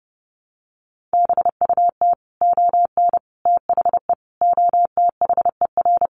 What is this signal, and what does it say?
Morse code, heard as a keyed tone, about 30 wpm, 700 Hz. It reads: BUT ON THE OTHER
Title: Sets of 5 Words 30wpm